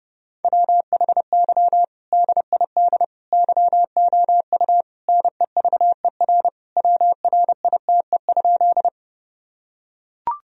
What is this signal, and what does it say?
Morse code, heard as a keyed tone, about 30 wpm, 700 Hz. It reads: WHY DID YOU NEVER WRITE?
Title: Sentences from Top 1000 Words - Review Part 1 - 30wpm